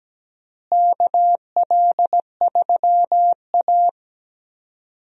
Morse code, heard as a keyed tone, about 17 wpm, 700 Hz. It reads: KL3A
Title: USA Call Signs - 4 Characters - Repeated 2x - 17wpm